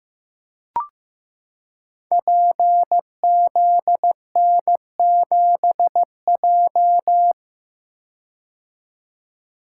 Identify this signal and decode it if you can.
Morse code, heard as a keyed tone, about 15 wpm, 700 Hz. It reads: PZN7J
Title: Code Groups - 5 Letter:Numbers - 15wpm